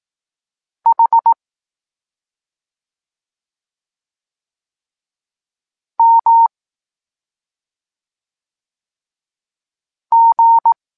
Morse code, heard as a keyed tone, about 18 wpm, 920 Hz. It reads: H M G
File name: Tekst-23 4